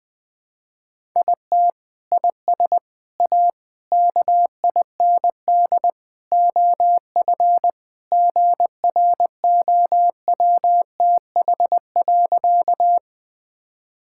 Morse code, heard as a keyed tone, about 20 wpm, 700 Hz. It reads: IT IS A KIND OF GROWTH.
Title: Sentences from Top 1000 Words - Review Part 5 - 20wpm